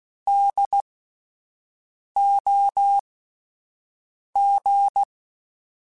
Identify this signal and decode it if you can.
Morse code, heard as a keyed tone, about 16 wpm, spaced as for 5 wpm, 800 Hz. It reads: DOG
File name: Spaced Alphas / Spaced Alphas mp3s / Ds sp @10